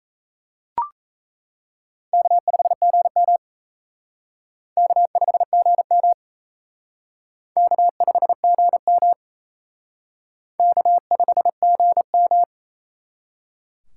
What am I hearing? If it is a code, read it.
Morse code, heard as a keyed tone, about 35 wpm, 700 Hz. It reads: K5GM K5GM K5GM K5GM